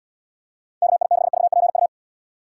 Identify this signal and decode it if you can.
Morse code, heard as a keyed tone, about 71 wpm, 700 Hz. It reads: KI6VPU